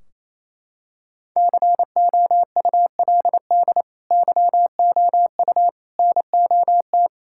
Morse code, heard as a keyed tone, about 28 wpm, 700 Hz. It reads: COULD YOU NOT
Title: Sentences from Top 1600 Words 28wpm